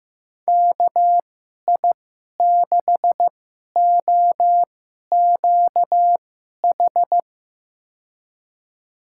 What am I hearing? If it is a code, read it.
Morse code, heard as a keyed tone, about 15 wpm, spaced as for 11 wpm, 700 Hz. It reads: KI6OQH
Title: USA Call Signs - 6 Characters - 2x Character Spacing - 15wpm